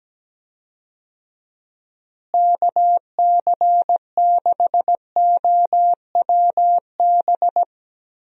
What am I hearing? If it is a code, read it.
Morse code, heard as a keyed tone, about 17 wpm, 700 Hz. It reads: KC6OWB
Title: USA Call Signs - 6 Characters - Repeated 3x - 17wpm